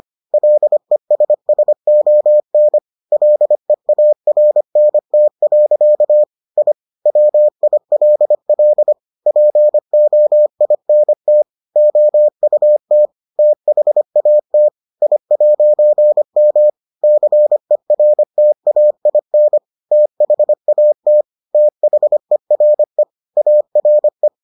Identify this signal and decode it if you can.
Morse code, heard as a keyed tone, about 25 wpm, 600 Hz. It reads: LESSON LEARNT. I WILL POINT OUT THAT I'M CERTAIN THAT THERE ARE